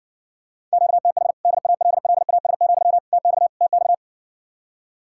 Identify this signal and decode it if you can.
Morse code, heard as a keyed tone, about 60 wpm, 700 Hz. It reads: QTH DALLAS, TX TX